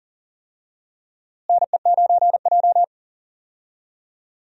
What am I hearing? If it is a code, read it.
Morse code, heard as a keyed tone, about 40 wpm, 700 Hz. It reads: NE9J
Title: USA Call Signs - 4 Characters - Repeated 3x - 40wpm